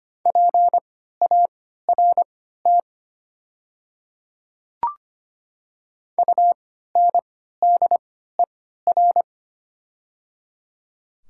Morse code, heard as a keyed tone, about 25 wpm, spaced as for 14 wpm, 690 Hz. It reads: PART UNDER
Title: 3x Character Spacing - Top 300 Words - 25wpm